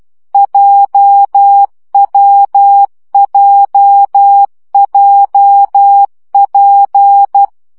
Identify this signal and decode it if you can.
Morse code, heard as a keyed tone, about 12 wpm, 800 Hz. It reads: JWJJP
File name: CW gyakorlas w p j 10 baud